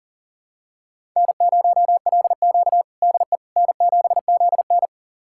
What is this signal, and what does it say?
Morse code, heard as a keyed tone, about 40 wpm, 700 Hz. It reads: N0LQ DE N7ZN
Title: USA Callsign DE Callsign - 4 Characters - 40wpm